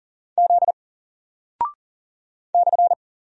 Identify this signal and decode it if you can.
Morse code, heard as a keyed tone, about 40 wpm, 700 Hz. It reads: Z /
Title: Single Character (Letter:Number:Punctuation:Prosign) Alt 40wpm